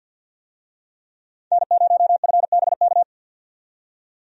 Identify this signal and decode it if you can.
Morse code, heard as a keyed tone, about 52 wpm, 700 Hz. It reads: N0FBK
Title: USA Call Signs - 5 Characters - 50wpm